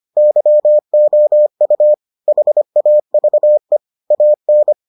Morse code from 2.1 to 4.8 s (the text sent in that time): HAVE AN